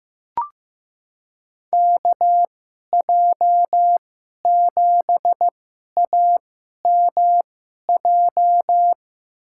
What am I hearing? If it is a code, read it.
Morse code, heard as a keyed tone, about 15 wpm, spaced as for 11 wpm, 700 Hz. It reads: KJ7AMJ
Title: USA Call Signs - 2x Character Spacing - 15wpm